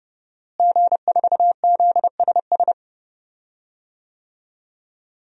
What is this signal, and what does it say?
Morse code, heard as a keyed tone, about 30 wpm, 700 Hz. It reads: G4ZSS
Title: DXCC-100 Call Signs - 5 Characters - Repeated 2x - 30wpm